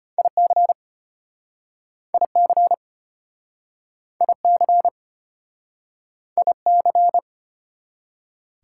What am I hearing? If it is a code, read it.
Morse code, heard as a keyed tone, about 34 wpm, 700 Hz. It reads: IC IC IC IC